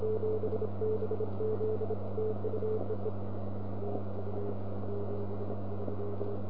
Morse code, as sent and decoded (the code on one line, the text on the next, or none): none